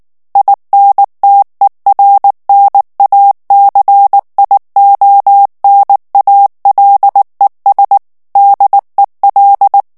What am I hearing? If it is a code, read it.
Morse code, heard as a keyed tone, about 19 wpm, 800 Hz. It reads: INTERNACIONALES DEL
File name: morse mp3 / ESPAGNOL TEXTE EN CLAIR 2 VITESSE 18 MOTS MINUTES